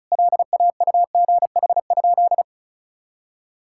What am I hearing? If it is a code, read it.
Morse code, heard as a keyed tone, about 34 wpm, 700 Hz. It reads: LAUGH?
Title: Sentences from Top 1000 Words - Review Part 5 - 35wpm